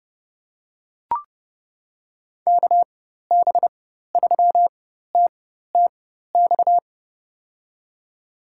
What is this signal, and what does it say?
Morse code, heard as a keyed tone, about 31 wpm, spaced as for 14 wpm, 700 Hz. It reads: KB3TTX
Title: USA Call Signs - 6 Characters - 4x Character Spacing - 30wpm